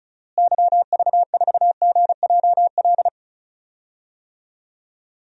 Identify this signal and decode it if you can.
Morse code, heard as a keyed tone, about 34 wpm, 700 Hz. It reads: YV4GJL
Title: DXCC-100 Call Signs - 6 Characters - Repeated 2x - 35wpm